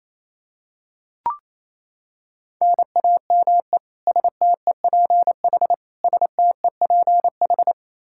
Morse code, heard as a keyed tone, about 28 wpm, 700 Hz. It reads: NAME STEPH STEPH
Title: Practice QSO - Exchange 1 - Beginning - 28wpm